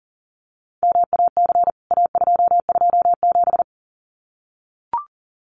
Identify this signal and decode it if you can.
Morse code, heard as a keyed tone, about 40 wpm, 700 Hz. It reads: MAC A227
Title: CWT Exchange 40wpm